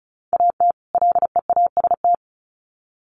Morse code, heard as a keyed tone, about 34 wpm, 700 Hz. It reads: AT LEAST
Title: Sets of 2 Words 35wpm